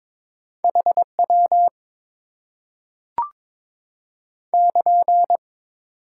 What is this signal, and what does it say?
Morse code, heard as a keyed tone, about 22 wpm, 700 Hz. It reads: HW <KN>
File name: Top 90 QSO Elements - 6x Character Spacing - 22wpm